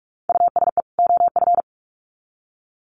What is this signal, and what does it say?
Morse code, heard as a keyed tone, about 44 wpm, 700 Hz. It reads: USE OF
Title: Sets of 2 Words 45wpm